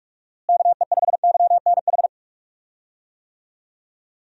Morse code, heard as a keyed tone, about 46 wpm, 700 Hz. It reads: KE5YNH
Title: USA Call Signs - 6 Characters - Repeated 3x - 45wpm